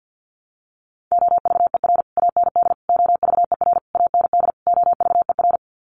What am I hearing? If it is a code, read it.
Morse code, heard as a keyed tone, about 50 wpm, 700 Hz. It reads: OVER AND OVER AND OVER
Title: Sets of 5 Words Encore 50wpm